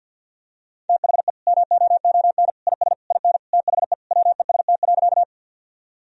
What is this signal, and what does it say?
Morse code, heard as a keyed tone, about 50 wpm, 700 Hz. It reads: THE MOON IS IN THE WEST.
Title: Sentences from Top 1000 Words - Review Part 3 - 50wpm